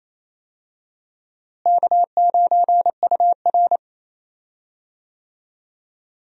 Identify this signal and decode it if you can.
Morse code, heard as a keyed tone, about 28 wpm, 700 Hz. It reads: K9UR
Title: USA Call Signs - 4 Characters - Repeated 2x - 28wpm